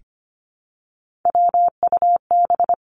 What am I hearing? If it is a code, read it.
Morse code, heard as a keyed tone, about 25 wpm, 700 Hz. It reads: WUB
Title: Three Challenging Letter-Number Combinations 25wpm